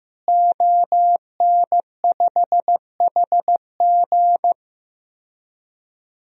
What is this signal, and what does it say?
Morse code, heard as a keyed tone, about 15 wpm, 700 Hz. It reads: ON5HG
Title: DXCC-100 Call Signs - 5 Characters - Repeated 3x - 15wpm